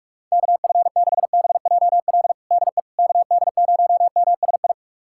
Morse code, heard as a keyed tone, about 46 wpm, 700 Hz. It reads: KF6BJL DE KD0MSI
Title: USA Callsign DE Callsign - 6 Characters - 45wpm